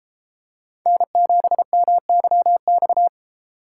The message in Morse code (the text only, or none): N7MYX